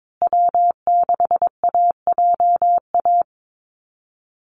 Morse code, heard as a keyed tone, about 22 wpm, 700 Hz. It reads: W6AJA
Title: USA Call Signs Repeated 2x 22wpm